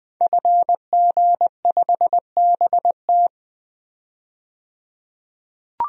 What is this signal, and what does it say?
Morse code, heard as a keyed tone, about 20 wpm, 700 Hz. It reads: FG5BT
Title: DXCC-100 Call Signs - 5 Characters - 20wpm